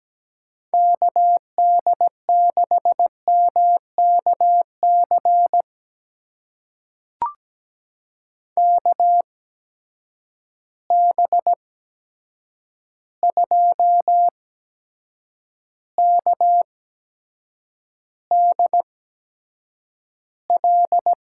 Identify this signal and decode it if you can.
Morse code, heard as a keyed tone, about 17 wpm, spaced as for 7 wpm, 700 Hz. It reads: KD6MKC KB2KDL
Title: USA Call Signs - 6 Characters - 8x Character Spacing - 17wpm